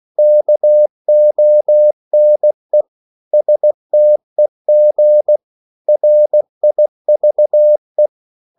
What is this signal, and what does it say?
Morse code, heard as a keyed tone, about 16 wpm, 600 Hz. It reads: KONE STEG RIVE
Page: Korte danske ord 16wpm